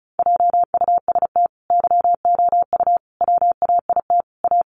Morse code, heard as a keyed tone, about 34 wpm, 700 Hz. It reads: JUST YOU WAIT A
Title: Sentences from Top 900 Words 35wpm